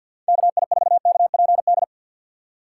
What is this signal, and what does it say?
Morse code, heard as a keyed tone, about 52 wpm, 700 Hz. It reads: KI4KPD
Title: USA Call Signs - 6x Character Spacing - 50wpm